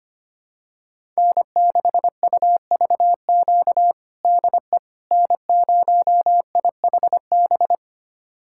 Morse code, heard as a keyed tone, about 25 wpm, 700 Hz. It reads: N6UVQ DE N0IHB
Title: USA Callsign DE Callsign - 5 Characters - 25wpm